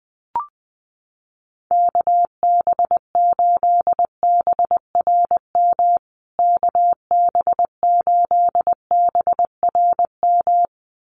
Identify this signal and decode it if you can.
Morse code, heard as a keyed tone, about 19 wpm, 700 Hz. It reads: KB8BRM KB8BRM
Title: USA Call Signs Repeated 2x 20wpm